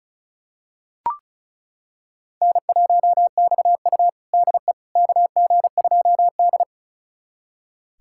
Morse code, heard as a keyed tone, about 36 wpm, 700 Hz. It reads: N1XU DE KG2D
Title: USA Callsign DE Callsign - 4 Characters - 35wpm